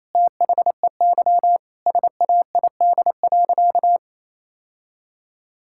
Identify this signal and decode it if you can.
Morse code, heard as a keyed tone, about 28 wpm, 700 Hz. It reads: THEY SAID.
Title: Sentences from Top 1600 Words 28wpm